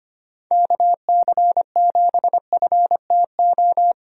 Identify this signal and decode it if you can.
Morse code, heard as a keyed tone, about 25 wpm, 700 Hz. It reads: KC7FTO
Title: USA Call Signs - 6 Characters - Repeated 2x - 25wpm